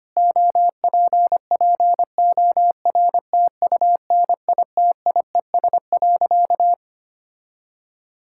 Morse code, heard as a keyed tone, about 25 wpm, 700 Hz. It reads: OPPORTUNITIES.
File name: practice / 25wpm / Difficult Sentences with Hints - 25wpm